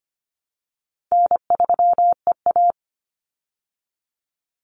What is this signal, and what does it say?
Morse code, heard as a keyed tone, about 25 wpm, 700 Hz. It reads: N3EA
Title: Easy Call Signs Repeated 2x - 25wpm